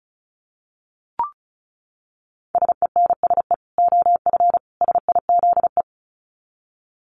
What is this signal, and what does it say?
Morse code, heard as a keyed tone, about 34 wpm, 700 Hz. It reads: SENSE OF SIZE
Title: Dit Challenge 35wpm